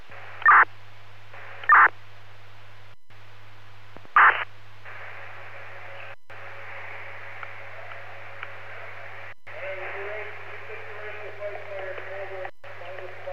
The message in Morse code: none

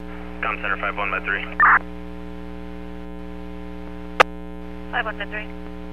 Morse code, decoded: none

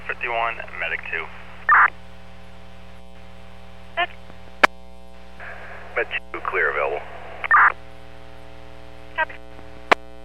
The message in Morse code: none

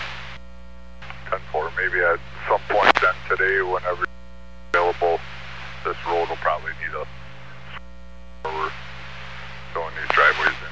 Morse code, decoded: none

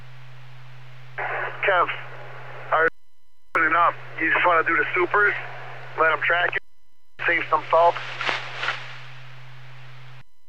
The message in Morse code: none

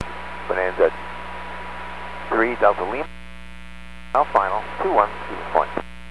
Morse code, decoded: none

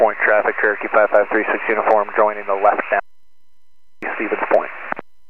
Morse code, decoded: none

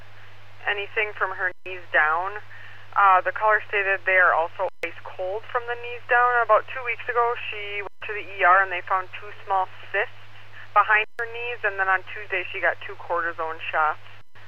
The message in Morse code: none

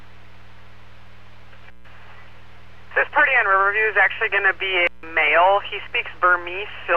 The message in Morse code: none

